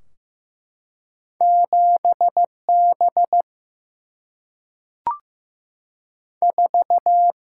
Morse code, heard as a keyed tone, about 15 wpm, 700 Hz. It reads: 7 B 4
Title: Single-Letter Single-Number 15wpm